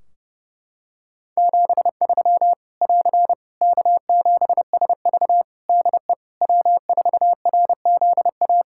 Morse code, heard as a keyed tone, about 30 wpm, 700 Hz. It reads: 73 <AR> K7SV DE W4RZA
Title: Practice QSO - Exchange 4 - Ending - 30wpm